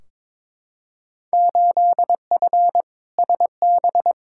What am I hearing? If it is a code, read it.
Morse code, heard as a keyed tone, about 22 wpm, 700 Hz. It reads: 8F SB
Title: Field Day Exchanges Encore - 22wpm